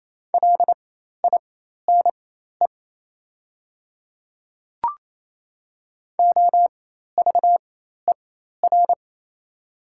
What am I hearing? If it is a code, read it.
Morse code, heard as a keyed tone, about 28 wpm, spaced as for 13 wpm, 700 Hz. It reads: LINE OVER
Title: 4x Character Spacing - Top 200 Words - 28wpm